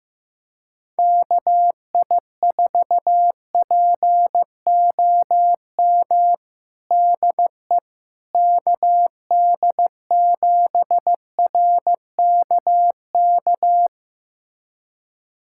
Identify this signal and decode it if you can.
Morse code, heard as a keyed tone, about 15 wpm, 700 Hz. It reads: KI4POM DE KD7RKK
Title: USA Callsign DE Callsign - 6 Characters - 15wpm